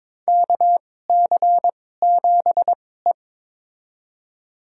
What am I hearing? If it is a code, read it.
Morse code, heard as a keyed tone, about 21 wpm, spaced as for 16 wpm, 700 Hz. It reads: KC7E
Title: USA Call Signs - 4 Characters - 2x Character Spacing - 22wpm